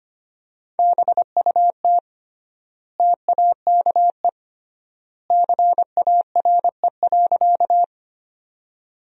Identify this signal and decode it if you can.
Morse code, heard as a keyed tone, about 25 wpm, 700 Hz. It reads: BUT TAKE CARE.
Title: 3x Word-Spacing - Sentences from Top 300 Words - 25wpm